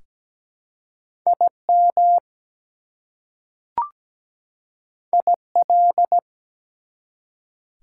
Morse code, heard as a keyed tone, about 17 wpm, 700 Hz. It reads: IM IL
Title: Prefixes and Suffixes 17wpm